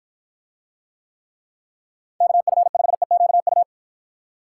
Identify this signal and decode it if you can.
Morse code, heard as a keyed tone, about 49 wpm, 700 Hz. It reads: KF5EQU